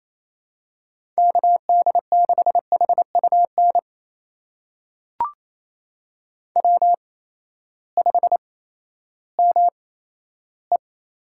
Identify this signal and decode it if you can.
Morse code, heard as a keyed tone, about 27 wpm, spaced as for 14 wpm, 700 Hz. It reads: KD6HUN W5ME
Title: USA Call Signs - 8x Character Spacing - 28wpm